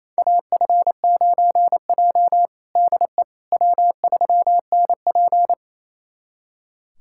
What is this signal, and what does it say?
Morse code, heard as a keyed tone, about 28 wpm, 700 Hz. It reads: AF9J DE W3NP
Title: USA Callsign DE Callsign - 4 Characters - 28wpm